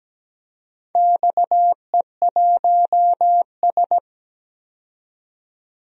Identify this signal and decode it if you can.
Morse code, heard as a keyed tone, about 17 wpm, 700 Hz. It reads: XE1S
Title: DXCC-100 Call Signs - 4 Characters - 17wpm